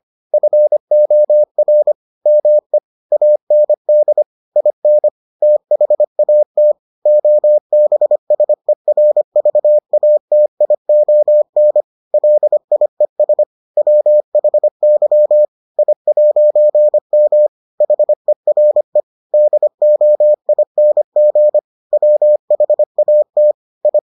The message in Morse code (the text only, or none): FOR ME AND IN THAT OBSERVATION LIES WHY I'M HERE DOING WHAT I